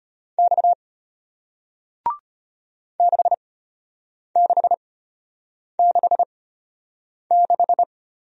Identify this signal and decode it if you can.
Morse code, heard as a keyed tone, about 34 wpm, spaced as for 8 wpm, 700 Hz. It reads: X 6666